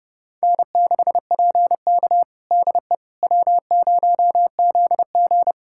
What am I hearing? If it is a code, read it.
Morse code, heard as a keyed tone, about 32 wpm, 700 Hz. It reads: N6PK DE W0ZG
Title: USA Callsign DE Callsign - 4 Characters - 30wpm